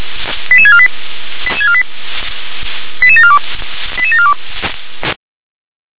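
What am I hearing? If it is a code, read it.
Five-tone selective call (ZVEI), heard as quick runs of five tones, tones of about 70 ms. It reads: 88648 88742 88742
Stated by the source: A ZVEI five-tone selective call, code 88088, received with an SDR.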